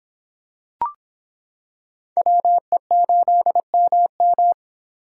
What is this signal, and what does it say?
Morse code, heard as a keyed tone, about 26 wpm, 700 Hz. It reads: WE8MM